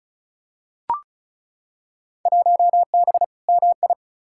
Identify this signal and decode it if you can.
Morse code, heard as a keyed tone, about 34 wpm, 700 Hz. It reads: 1B MI
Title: Field Day Exchanges - 35wpm